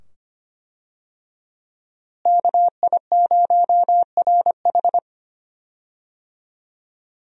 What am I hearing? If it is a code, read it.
Morse code, heard as a keyed tone, about 25 wpm, 700 Hz. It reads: KI0RH
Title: USA Call Signs - 5 Characters - Repeated 2x - 25wpm